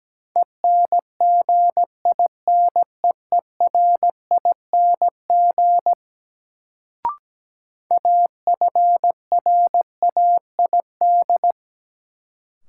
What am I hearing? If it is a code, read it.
Morse code, heard as a keyed tone, about 17 wpm, 700 Hz. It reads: ENGINEERING AFRAID
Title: Sentences from Top 1500 Words - New Word Drill - 17wpm